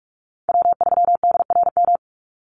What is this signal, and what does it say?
Morse code, heard as a keyed tone, about 44 wpm, 700 Hz. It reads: W3DRM
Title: USA Call Signs 45wpm